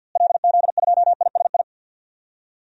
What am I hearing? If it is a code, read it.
Morse code, heard as a keyed tone, about 52 wpm, 700 Hz. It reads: LZ2ISI
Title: DXCC-100 Call Signs - 6 Characters - Repeated 3x - 50wpm